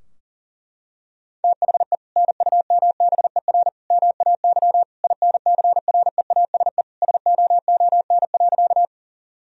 Morse code, heard as a keyed tone, about 40 wpm, 700 Hz. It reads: THE NUMBER MAY INCREASE SOON.
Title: Sentences from Top 1000 Words - Review Part 4 - 40wpm